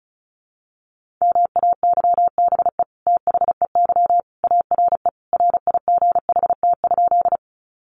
Morse code, heard as a keyed tone, about 34 wpm, 700 Hz. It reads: MAYBE THEY ARE RIGHT?
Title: Sentences from Top 1000 Words 35wpm